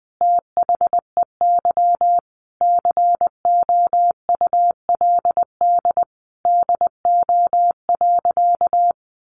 Morse code, heard as a keyed tone, about 20 wpm, 700 Hz. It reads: THEY COULD DO.
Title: Sentences from Top 700 Words 20wpm